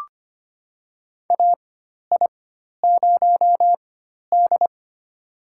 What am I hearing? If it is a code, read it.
Morse code, heard as a keyed tone, about 25 wpm, spaced as for 12 wpm, 700 Hz. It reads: AI0D
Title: USA Call Signs - 4 Characters - 4x Character Spacing - 25wpm